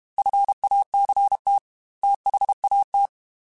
Morse code, heard as a keyed tone, about 32 wpm, 800 Hz. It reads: FACT THAT
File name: Technical Subjects / Technical Subjects-mp3s / Reactance @30